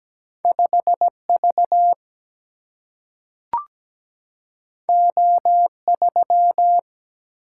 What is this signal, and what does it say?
Morse code, heard as a keyed tone, about 17 wpm, 700 Hz. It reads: 5 V O 3
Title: Single-Letter Single-Number 17wpm